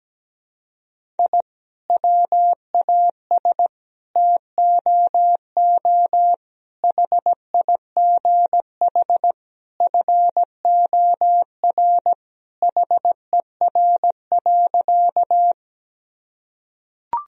Morse code, heard as a keyed tone, about 17 wpm, 700 Hz. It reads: I WAS TOO HIGH FOR HER.
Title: Sentences from Top 1000 Words - Review Part 1 - 17wpm